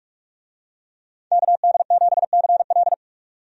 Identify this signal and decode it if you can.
Morse code, heard as a keyed tone, about 44 wpm, 700 Hz. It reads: KD7CL
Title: USA Call Signs - 5 Characters - 45wpm